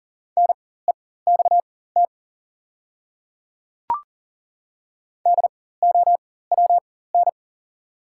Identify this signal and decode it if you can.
Morse code, heard as a keyed tone, about 42 wpm, spaced as for 19 wpm, 690 Hz. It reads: NEXT DOWN
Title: 4x Character Spacing - Top 300 Words - 40wpm